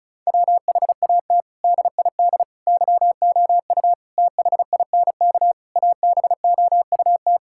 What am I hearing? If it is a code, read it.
Morse code, heard as a keyed tone, about 34 wpm, 700 Hz. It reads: WHAT DID YOU THINK ABOUT
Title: Sentences from Top 1000 Words - Review Part 5 - 35wpm